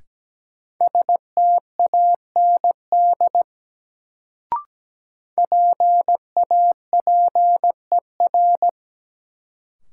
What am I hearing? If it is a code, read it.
Morse code, heard as a keyed tone, about 17 wpm, 700 Hz. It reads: STAND PAPER
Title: Sentences from Top 300 Words - New Word Drill - 17wpm